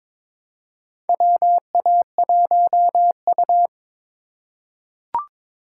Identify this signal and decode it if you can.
Morse code, heard as a keyed tone, about 22 wpm, 700 Hz. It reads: WA1U
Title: USA Call Signs - 4 Characters - 3x Character Spacing - 22wpm